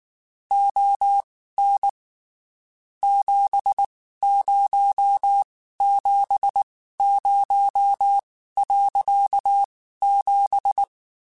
Morse code, heard as a keyed tone, about 19 wpm, spaced as for 14 wpm, 800 Hz. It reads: ON 7070.7
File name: (spaced) / QSOs Spaced-mp3s / WA7QOX 2 @18 sp